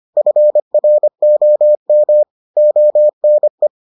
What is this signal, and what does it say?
Morse code, heard as a keyed tone, about 25 wpm, 600 Hz. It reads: FROM ONE